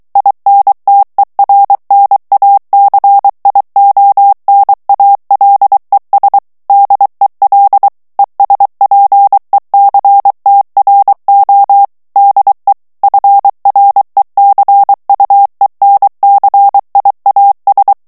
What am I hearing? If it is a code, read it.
Morse code, heard as a keyed tone, about 23 wpm, 800 Hz. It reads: INTERNACIONALES DEL ESPECTRO DE FRECUENCIAS
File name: morse mp3 / ESPAGNOL TEXTE EN CLAIR 2 VITESSE 22 MOTS MINUTES